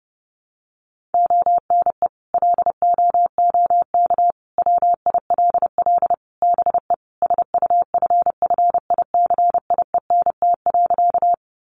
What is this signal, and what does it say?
Morse code, heard as a keyed tone, about 28 wpm, 700 Hz. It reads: ONE LOOK WILL BE SUFFICIENT.
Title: Sentences from Top 1400 Words 30wpm